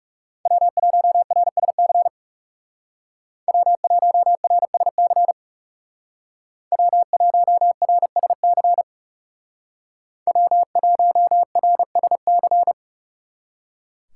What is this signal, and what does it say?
Morse code, heard as a keyed tone, about 39 wpm, 700 Hz. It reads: W1RSC W1RSC W1RSC W1RSC